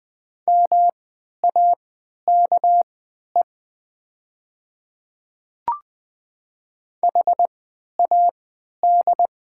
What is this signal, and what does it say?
Morse code, heard as a keyed tone, about 20 wpm, spaced as for 11 wpm, 700 Hz. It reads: MAKE HAD
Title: 3x Character Spacing - Top 100 Words - 20wpm